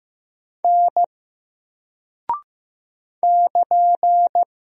Morse code, heard as a keyed tone, about 15 wpm, 700 Hz. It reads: N <KN>
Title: Single Character (Letter:Number:Punctuation:Prosign) 15wpm